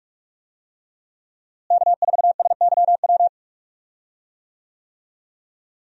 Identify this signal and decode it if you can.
Morse code, heard as a keyed tone, about 46 wpm, 700 Hz. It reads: K4SYW
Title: USA Call Signs - 5 Characters - Repeated 3x - 45wpm